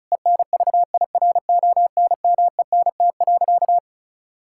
Morse code, heard as a keyed tone, about 36 wpm, 700 Hz. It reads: ENVIRONMENT.
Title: Sentences from Top 1000 Words - Review Part 5 - 35wpm